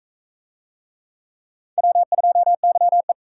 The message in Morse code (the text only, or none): W2YE